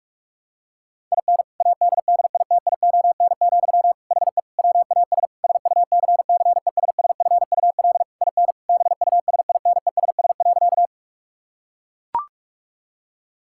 Morse code, heard as a keyed tone, about 41 wpm, 700 Hz. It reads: IN ADDITION, HE WAS SUCCESSFUL IN BUSINESS.
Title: Sentences from Top 1000 Words - Review Part 6 - 45wpm